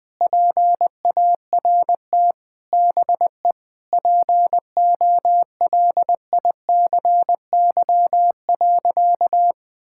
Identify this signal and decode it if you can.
Morse code, heard as a keyed tone, about 20 wpm, 700 Hz. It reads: PART BE POLICY.
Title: Sentences from Top 1000 Words - Review Part 5 - 20wpm